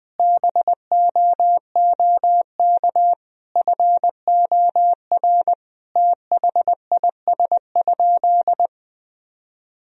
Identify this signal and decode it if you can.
Morse code, heard as a keyed tone, about 20 wpm, 700 Hz. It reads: BOOK FOR THIS?
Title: Sentences from Top 1000 Words - Review Part 4 - 20wpm